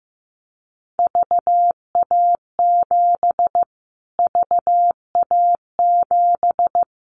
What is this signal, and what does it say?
Morse code, heard as a keyed tone, about 15 wpm, 700 Hz. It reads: VA7 VA7
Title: Callsign Prefixes for Top 25 DXCC - Repeated 3x - 15wpm